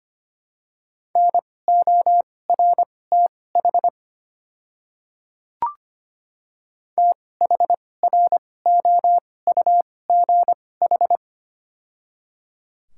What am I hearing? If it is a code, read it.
Morse code, heard as a keyed tone, about 25 wpm, spaced as for 18 wpm, 690 Hz. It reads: NORTH THROUGH
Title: 2x Character Spacing - Top 500 Words - 25wpm